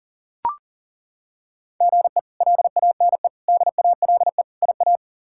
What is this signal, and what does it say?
Morse code, heard as a keyed tone, about 40 wpm, 700 Hz. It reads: GE LANE DALE IA
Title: K1USN SST QSO Practice - 40wpm